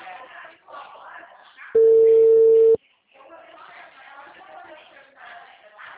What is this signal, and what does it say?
Ringback tone (China, 450 Hz), with speech.
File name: out- -2010- - - .21169